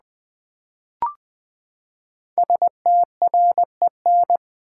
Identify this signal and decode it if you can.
Morse code, heard as a keyed tone, about 20 wpm, 700 Hz. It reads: STREN
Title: Build A Word - 5 Letters At A Time - 10-Letter Words - 20wpm